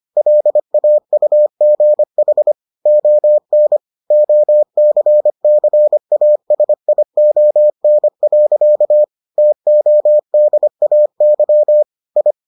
Morse code, heard as a keyed tone, about 25 wpm, 600 Hz. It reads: LAUGH ON OCCASION. TODAY I